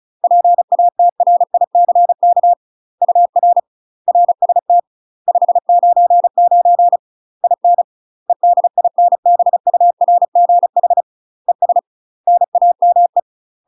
Morse code, heard as a keyed tone, about 34 wpm, 700 Hz. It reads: PATRICK UR RST 599 IN EDINBURGH ES NAME
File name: UK MASTER QSOs TRAINING / 35WPM / UK MASTER QSOs 0021